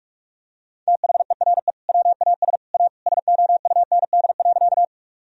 Morse code, heard as a keyed tone, about 47 wpm, 700 Hz. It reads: THERE WAS A SOUND.
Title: Sentences from Top 1000 Words - Review Part 1 - 45wpm